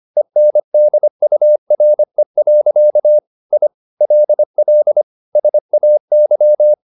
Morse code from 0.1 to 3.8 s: ENDURE. I

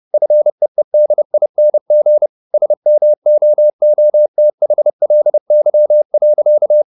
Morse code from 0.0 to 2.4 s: FEEDING

